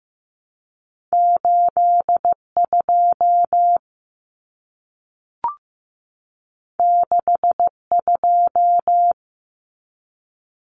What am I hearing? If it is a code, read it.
Morse code, heard as a keyed tone, about 15 wpm, 700 Hz. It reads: 82 62
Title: Common Cut Numbers - 2-Digit 15wpm